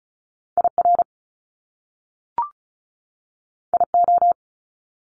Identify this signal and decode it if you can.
Morse code, heard as a keyed tone, about 36 wpm, 700 Hz. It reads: IR IO
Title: Top Two Letter Combinations in Words 35wpm